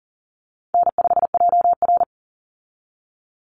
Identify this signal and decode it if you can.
Morse code, heard as a keyed tone, about 40 wpm, 700 Hz. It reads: N5JR
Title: Call Signs Repeated 2x - 40wpm